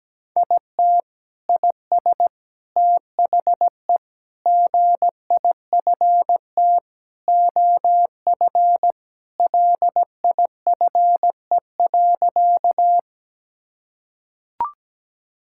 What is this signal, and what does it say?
Morse code, heard as a keyed tone, about 17 wpm, 700 Hz. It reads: IT IS THE GIFT OF LIFE.
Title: Sentences from Top 1000 Words - Review Part 6 - 17wpm